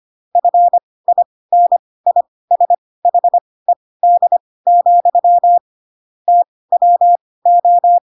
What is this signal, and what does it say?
Morse code, heard as a keyed tone, about 25 wpm, spaced as for 18 wpm, 700 Hz. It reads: FINISHED, TWO